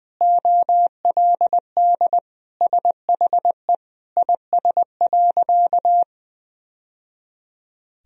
Morse code, heard as a keyed tone, about 20 wpm, 700 Hz. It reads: OLD SHE IS.
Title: Sentences from Top 1000 Words - Review Part 3 - 20wpm